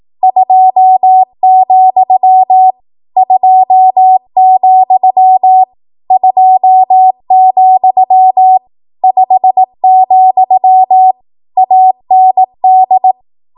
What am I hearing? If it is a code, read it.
Morse code, heard as a keyed tone, about 18 wpm, 750 Hz. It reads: 2, 2, 2, 5, AND
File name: Morse / 18WPM